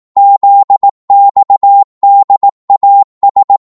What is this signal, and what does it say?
Morse code, heard as a keyed tone, about 18 wpm, 800 Hz. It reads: ZXDAS
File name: Morse / ltr-18wpm